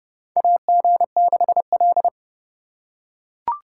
Morse code, heard as a keyed tone, about 30 wpm, 700 Hz. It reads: AG6L
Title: USA Call Signs - 4 Characters - 3x Character Spacing - 30wpm